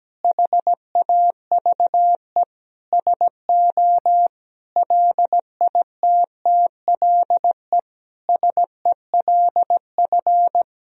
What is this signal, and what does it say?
Morse code, heard as a keyed tone, about 17 wpm, 700 Hz. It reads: HAVE SO LITTLE SELF
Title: Sentences from Top 1000 Words - Review Part 1 - 17wpm